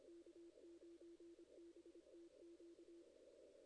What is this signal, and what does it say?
Morse code, heard as a keyed tone, about 25 wpm, 340 Hz. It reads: K9BTQ
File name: 2016 JAN CW NAQP / 2016 Jan naqp k5ot - 0-8